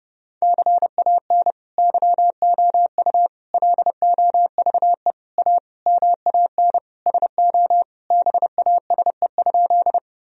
Morse code, heard as a keyed tone, about 30 wpm, 700 Hz. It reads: CAN YOU LOVE A MAN SO BASE?
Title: Sentences from Top 1000 Words - Review Part 1 - 30wpm